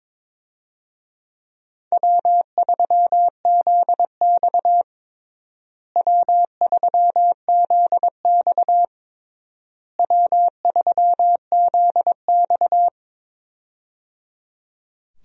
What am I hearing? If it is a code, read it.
Morse code, heard as a keyed tone, about 22 wpm, 700 Hz. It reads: W3ZX W3ZX W3ZX
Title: USA Call Signs - 4 Characters - Repeated 3x - 22wpm